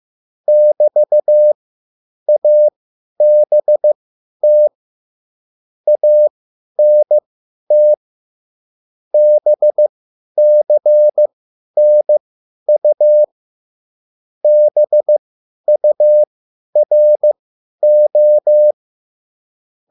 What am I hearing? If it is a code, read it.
Morse code, heard as a keyed tone, about 15 wpm, spaced as for 10 wpm, 600 Hz. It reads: <BT>ABT ANT BCNU BURO